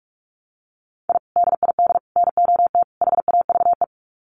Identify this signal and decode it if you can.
Morse code, heard as a keyed tone, about 44 wpm, 700 Hz. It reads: I DID NOT HAVE
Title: Sets of 4 Words 45wpm